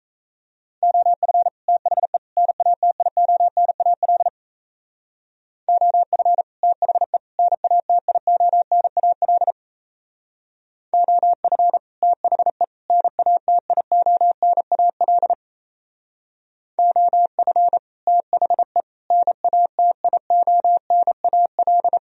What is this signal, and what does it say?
Morse code, heard as a keyed tone, about 35 wpm, 700 Hz. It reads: OF THE NATIONAL OF THE NATIONAL OF THE NATIONAL OF THE NATIONAL
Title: Speed-Racing - Sets of 3 Words 28wpm